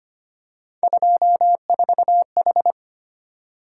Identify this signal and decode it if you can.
Morse code, heard as a keyed tone, about 25 wpm, 700 Hz. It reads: 24H
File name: Three Letter-Number Combinations - 25wpm